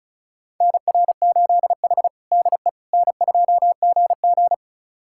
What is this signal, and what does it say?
Morse code, heard as a keyed tone, about 36 wpm, 700 Hz. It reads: NR8H DE N2GG
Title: USA Callsign DE Callsign - 4 Characters - 35wpm